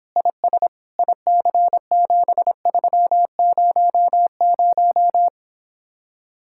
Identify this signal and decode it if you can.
Morse code, heard as a keyed tone, about 26 wpm, 700 Hz. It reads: IS IC7300